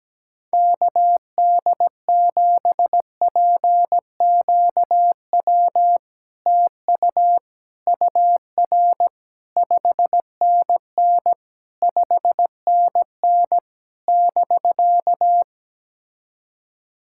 Morse code, heard as a keyed tone, about 17 wpm, 700 Hz. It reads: KD7PQW TU UR 5NN 5NN <BK>
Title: SOTA QSO Practice 17wpm